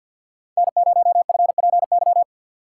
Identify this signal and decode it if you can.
Morse code, heard as a keyed tone, about 52 wpm, 690 Hz. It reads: N0FPY
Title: USA Call Signs - 2x Character Spacing - 50wpm